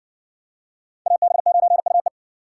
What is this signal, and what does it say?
Morse code, heard as a keyed tone, about 55 wpm, 700 Hz. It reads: AB9UE